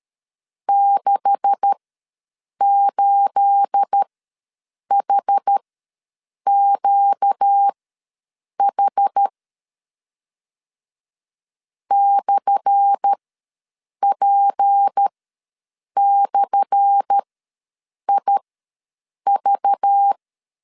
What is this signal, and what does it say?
Morse code, heard as a keyed tone, about 13 wpm, spaced as for 7 wpm, 800 Hz. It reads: 68HQH /P/IV
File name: LEZIONI / 10 SIGNS / 10A SLASH ws 12wpm 3443